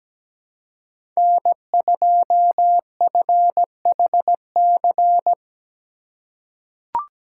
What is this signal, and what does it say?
Morse code, heard as a keyed tone, about 17 wpm, 700 Hz. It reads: N2FHC
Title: USA Call Signs - 4x Character Spacing - 17wpm